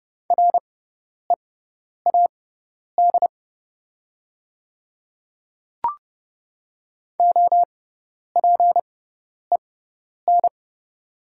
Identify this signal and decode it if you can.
Morse code, heard as a keyed tone, about 30 wpm, spaced as for 10 wpm, 690 Hz. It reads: READ OPEN
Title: 6x Character Spacing - Top 300 Words - 30wpm